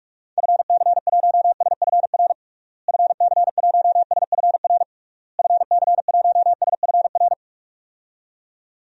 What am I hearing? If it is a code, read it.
Morse code, heard as a keyed tone, about 41 wpm, 700 Hz. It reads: FC1SFR FC1SFR FC1SFR
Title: DXCC-100 Call Signs - 6 Characters - Repeated 3x - 45wpm